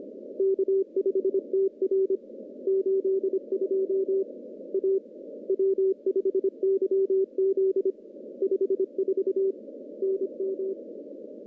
Morse code, heard as a keyed tone, about 25 wpm, 380 Hz. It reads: K5TR 82 A W5YZ 54 NM
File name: contests / CW-SS-k5tr- -1400